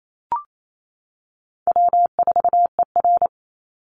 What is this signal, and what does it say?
Morse code, heard as a keyed tone, about 28 wpm, 700 Hz. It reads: W4ER
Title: Easy Call Signs - 28wpm